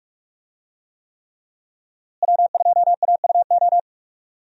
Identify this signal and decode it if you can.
Morse code, heard as a keyed tone, about 46 wpm, 700 Hz. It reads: W2AUO
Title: USA Call Signs - 5 Characters - Repeated 3x - 45wpm